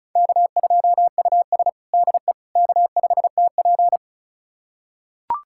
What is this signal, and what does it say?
Morse code, heard as a keyed tone, about 36 wpm, 700 Hz. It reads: K2US DE K5TP
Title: USA Callsign DE Callsign - 4 Characters - 35wpm